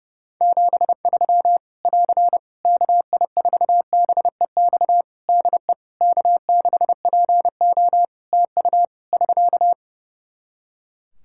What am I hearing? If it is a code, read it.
Morse code, heard as a keyed tone, about 32 wpm, 700 Hz. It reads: 73 <AR> KI4BEX DE K6PO TU <SK>
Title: Practice QSO - Exchange 4 - Ending - 30wpm